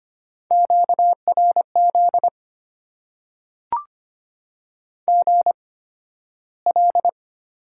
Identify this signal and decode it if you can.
Morse code, heard as a keyed tone, about 25 wpm, spaced as for 13 wpm, 700 Hz. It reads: QRZ GL
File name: Top 90 QSO Elements - 8x Character Spacing - 25wpm